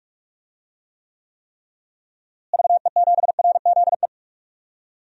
Morse code, heard as a keyed tone, about 41 wpm, 700 Hz. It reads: VE7RZE
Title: DXCC-100 Call Signs - 6 Characters - Repeated 3x - 45wpm